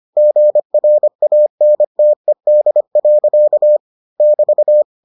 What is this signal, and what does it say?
Morse code, heard as a keyed tone, about 25 wpm, 600 Hz. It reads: GRANTED. <BT>